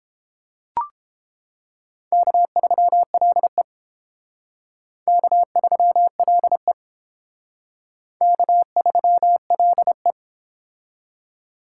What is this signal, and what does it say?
Morse code, heard as a keyed tone, about 29 wpm, 700 Hz. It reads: K3LE K3LE K3LE